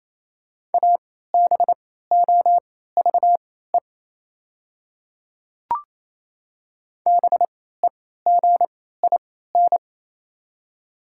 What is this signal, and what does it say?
Morse code, heard as a keyed tone, about 27 wpm, spaced as for 16 wpm, 700 Hz. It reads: ABOVE BEGIN
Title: 3x Character Spacing - Top 500 Words - 28wpm